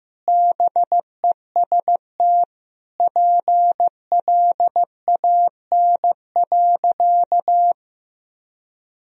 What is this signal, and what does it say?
Morse code, heard as a keyed tone, about 16 wpm, 700 Hz. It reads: BEST PLAN.
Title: Sentences from Top 1000 Words - Review Part 2 - 15wpm